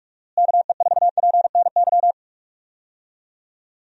Morse code, heard as a keyed tone, about 46 wpm, 700 Hz. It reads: KE4PNY
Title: USA Call Signs - 6 Characters - Repeated 3x - 45wpm